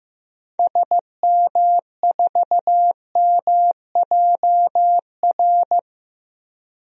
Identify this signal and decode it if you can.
Morse code, heard as a keyed tone, about 15 wpm, 700 Hz. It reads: SM4MJR
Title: DXCC-100 Call Signs - 6 Characters - Repeated 2x - 15wpm